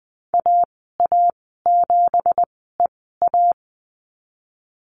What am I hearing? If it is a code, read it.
Morse code, heard as a keyed tone, about 20 wpm, spaced as for 15 wpm, 700 Hz. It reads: AA7EA
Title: 179 Lesson 29g - Slash - Call Signs 20wpm